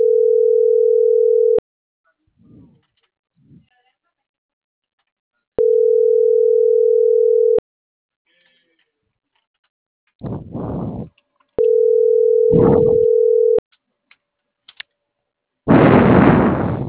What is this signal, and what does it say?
Ringback tone (North America), 440 + 480 Hz.